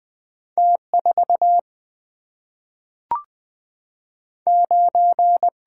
Morse code, heard as a keyed tone, about 20 wpm, 700 Hz. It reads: T4 9
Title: Single-Letter Single-Number 20wpm